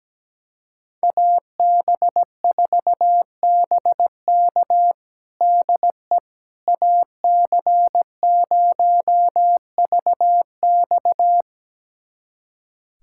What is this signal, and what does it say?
Morse code, heard as a keyed tone, about 17 wpm, 700 Hz. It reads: AB4BK DE AC0VX
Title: USA Callsign DE Callsign - 5 Characters - 17wpm